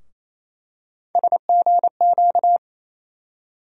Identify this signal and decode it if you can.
Morse code, heard as a keyed tone, about 28 wpm, 700 Hz. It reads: SGQ
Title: USA Call Sign Postfixes - 3 Letters 28wpm